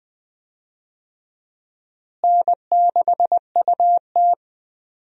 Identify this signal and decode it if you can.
Morse code, heard as a keyed tone, about 20 wpm, 700 Hz. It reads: N6UT
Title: USA Call Signs - 4 Characters - Repeated 2x - 20wpm